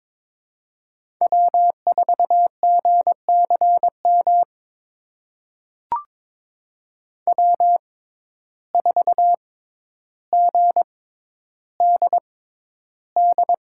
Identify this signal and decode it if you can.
Morse code, heard as a keyed tone, about 22 wpm, spaced as for 11 wpm, 700 Hz. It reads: W4GCM W4GDD
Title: USA Call Signs - 5 Characters - 6x Character Spacing - 22wpm